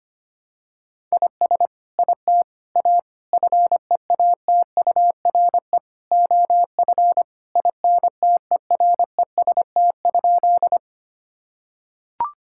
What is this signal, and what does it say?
Morse code, heard as a keyed tone, about 25 wpm, 700 Hz. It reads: IS IT A FEATURE OF INTEREST?
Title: Sentences from Top 1000 Words - Review Part 5 - 25wpm